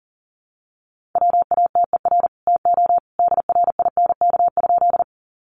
Morse code, heard as a keyed tone, about 40 wpm, 700 Hz. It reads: WATER TO DRINK?
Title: Sentences from Top 1500 Words 40wpm